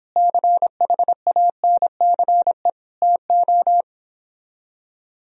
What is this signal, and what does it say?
Morse code, heard as a keyed tone, about 26 wpm, 700 Hz. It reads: CHANCE TO